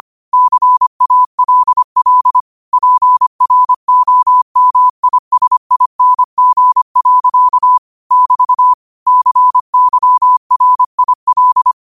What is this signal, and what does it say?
Morse code, heard as a keyed tone, about 25 wpm, 1000 Hz. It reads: CALL PROMISING. <BT> CYRIL